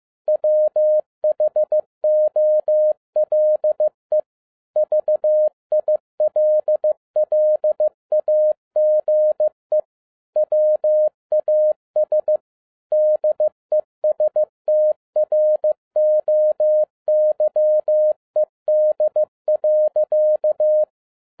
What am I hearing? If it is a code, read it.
Morse code, heard as a keyed tone, about 15 wpm, 610 Hz. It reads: WHOLE VILLAGE WAS DESTROYED.
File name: 2011-10-mp3 / n02-FIELD DAY CONTEST QRV FROM SAKAE VILLAGE 15wpm